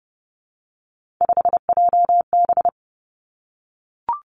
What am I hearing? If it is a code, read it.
Morse code, heard as a keyed tone, about 30 wpm, 700 Hz. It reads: 5JB
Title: Three Challenging Letter-Number Combinations 30wpm